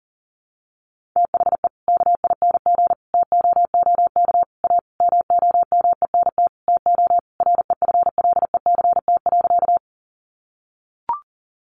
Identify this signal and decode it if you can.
Morse code, heard as a keyed tone, about 40 wpm, 700 Hz. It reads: THE KING TOOK A MOMENT TO REFLECT.
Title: Sentences from Top 1500 Words 40wpm